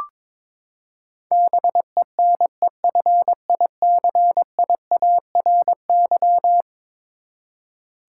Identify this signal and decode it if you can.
Morse code, heard as a keyed tone, about 22 wpm, 700 Hz. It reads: BENEFICIARY
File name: practice / 22wpm / Eleven Letter Words - 22wpm